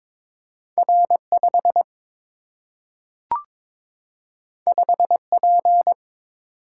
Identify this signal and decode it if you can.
Morse code, heard as a keyed tone, about 22 wpm, 700 Hz. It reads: R 5 5 P
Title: Single-Letter Single-Number 22wpm